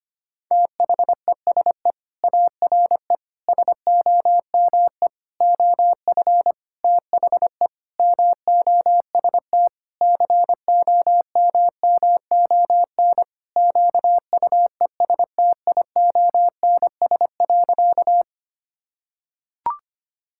Morse code, heard as a keyed tone, about 25 wpm, 700 Hz. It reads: THESE ARE SOME OF THE MOST COMMON QUESTIONS.
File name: practice / 25wpm / Sentences with 4-Word Phrases - 25wpm